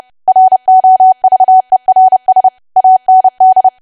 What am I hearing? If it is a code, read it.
Morse code, heard as a keyed tone, about 30 wpm, 750 Hz. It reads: ROVERS AND
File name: cwa / W1AW-Level3 / 30WPM